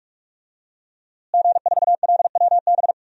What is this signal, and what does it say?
Morse code, heard as a keyed tone, about 41 wpm, 700 Hz. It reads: G4LWB
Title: DXCC-100 Call Signs - 5 Characters - Repeated 3x - 45wpm